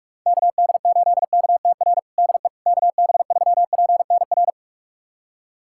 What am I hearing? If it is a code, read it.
Morse code, heard as a keyed tone, about 46 wpm, 700 Hz. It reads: KD8KTR DE KB3PNR
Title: USA Callsign DE Callsign - 6 Characters - 45wpm